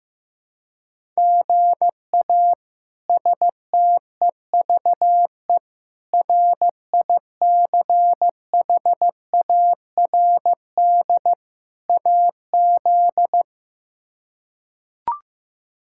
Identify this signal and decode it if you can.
Morse code, heard as a keyed tone, about 16 wpm, 700 Hz. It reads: GA STEVE RICHARD AZ
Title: K1USN SST QSO Practice - 15wpm